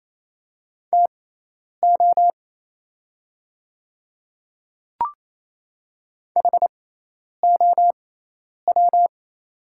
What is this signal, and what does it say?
Morse code, heard as a keyed tone, about 28 wpm, spaced as for 10 wpm, 690 Hz. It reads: TO HOW
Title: 6x Character Spacing - Top 200 Words - 28wpm